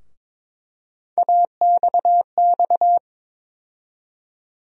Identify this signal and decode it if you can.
Morse code, heard as a keyed tone, about 22 wpm, 700 Hz. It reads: AXX
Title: USA Call Sign Postfixes - 3 Letters 22wpm